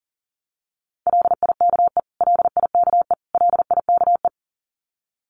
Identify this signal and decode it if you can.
Morse code, heard as a keyed tone, about 40 wpm, 700 Hz. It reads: LIKE LIKE LIKE
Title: Top 200 Words Repeated 40wpm